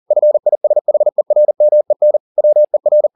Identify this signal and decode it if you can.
Morse code, heard as a keyed tone, about 40 wpm, 600 Hz. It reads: FISHERMEN WER
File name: the old man and the sea 2 40wpm